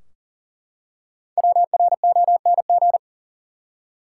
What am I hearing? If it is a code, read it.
Morse code, heard as a keyed tone, about 40 wpm, 700 Hz. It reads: WRONG
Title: Five Letter Words - 40wpm